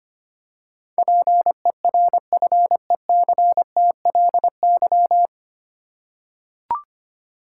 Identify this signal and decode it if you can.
Morse code, heard as a keyed tone, about 25 wpm, 700 Hz. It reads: PERFECTLY
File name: practice / 25wpm / 9 Letter Words - First 3 Letters Given - 25wpm